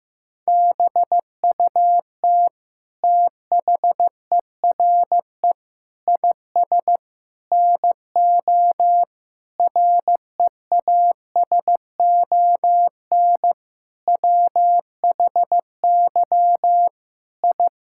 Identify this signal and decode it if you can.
Morse code, heard as a keyed tone, about 15 wpm, 700 Hz. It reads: BUT THERE IS NO REASON WHY I
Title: Sentences from Top 1000 Words - Review Part 6 - 15wpm